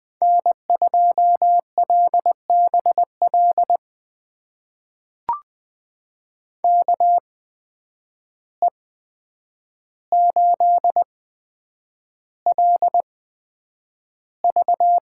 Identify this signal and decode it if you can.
Morse code, heard as a keyed tone, about 20 wpm, spaced as for 9 wpm, 700 Hz. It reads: N2LBL KE8LV
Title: USA Call Signs - 5 Characters - 8x Character Spacing - 20wpm